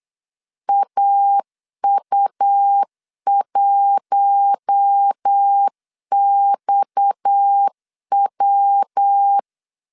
Morse code, heard as a keyed tone, about 8 wpm, 800 Hz. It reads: AU1XW
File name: LEZIONI / 6 PKX0 / 6 PKX 8wpm 3041